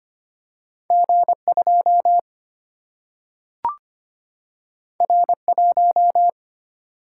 Morse code, heard as a keyed tone, about 25 wpm, 700 Hz. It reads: G2 R1
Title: Single-Letter Single-Number 25wpm